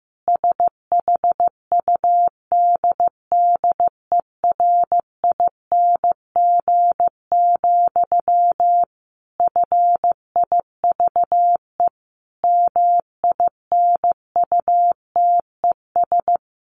Morse code, heard as a keyed tone, about 15 wpm, 700 Hz. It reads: SHUDDERING, FIVE MINUTES
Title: A Sound Of Thunder 15wpm